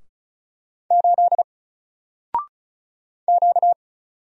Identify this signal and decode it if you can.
Morse code, heard as a keyed tone, about 36 wpm, 700 Hz. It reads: 8 Q
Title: Single Letter-Number Alt 35wpm